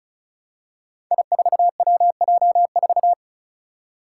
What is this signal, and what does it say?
Morse code, heard as a keyed tone, about 34 wpm, 700 Hz. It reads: I4WJ4
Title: Five Challenging Letter-Number Combinations 35wpm